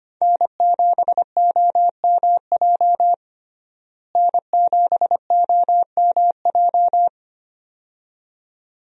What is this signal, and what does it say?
Morse code, heard as a keyed tone, about 25 wpm, 700 Hz. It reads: N7OMJ N7OMJ
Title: USA Call Signs - 5 Characters - Repeated 2x - 25wpm